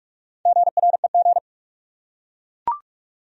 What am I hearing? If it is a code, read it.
Morse code, heard as a keyed tone, about 44 wpm, 700 Hz. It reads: GREG
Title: USA Common Names - 45wpm